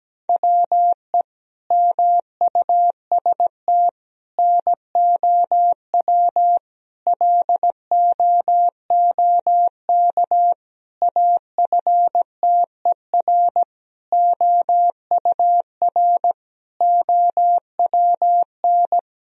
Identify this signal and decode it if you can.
Morse code, heard as a keyed tone, about 17 wpm, 700 Hz. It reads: WE MUST NOW LOOK AFTER OUR OWN
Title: Sentences from Top 1000 Words - Review Part 4 - 17wpm